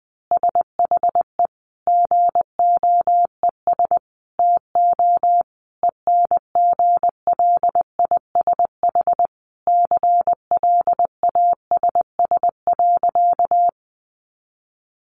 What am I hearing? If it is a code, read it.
Morse code, heard as a keyed tone, about 19 wpm, 700 Hz. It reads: SHE GOES TO ENGLISH CLASS.
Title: Additional Easy Phrases 20wpm